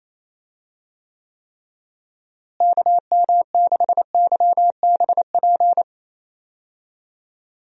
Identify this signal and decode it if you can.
Morse code, heard as a keyed tone, about 27 wpm, 700 Hz. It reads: KM6YBP
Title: USA Call Signs - 6 Characters - Repeated 3x - 28wpm